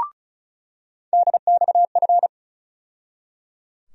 Morse code, heard as a keyed tone, about 36 wpm, 700 Hz. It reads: DXF
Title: USA Call Sign Postfixes - 3 Letters 35wpm